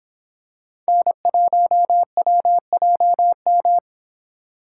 Morse code, heard as a keyed tone, about 26 wpm, 700 Hz. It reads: N1WJM